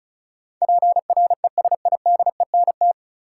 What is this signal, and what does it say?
Morse code, heard as a keyed tone, about 36 wpm, 700 Hz. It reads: PRESIDENT
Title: Sets of 2 Words - New Word Drill - 35wpm